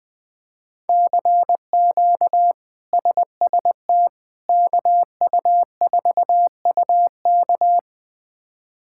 Morse code, heard as a keyed tone, about 20 wpm, 700 Hz. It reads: CQ SST KU4UK
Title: K1USN SST QSO Practice - 20wpm